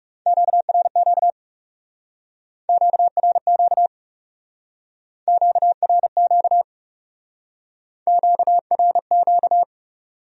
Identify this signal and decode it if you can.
Morse code, heard as a keyed tone, about 38 wpm, 700 Hz. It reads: QRQ QRQ QRQ QRQ
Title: Speed-Racing Q-Codes - 30wpm